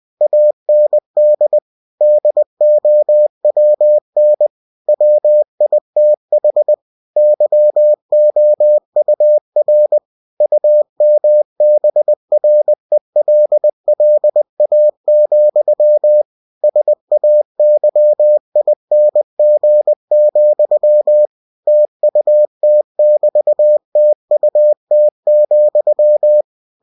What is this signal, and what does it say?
Morse code, heard as a keyed tone, about 20 wpm, 600 Hz. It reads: AND DOWN WITH YOUR UMBRELLA, SAYING, TUT<BT>TUT,